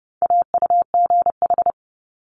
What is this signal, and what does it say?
Morse code, heard as a keyed tone, about 30 wpm, 700 Hz. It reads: AUGH
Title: Multi-letter Phonemes Quick Fire 30wpm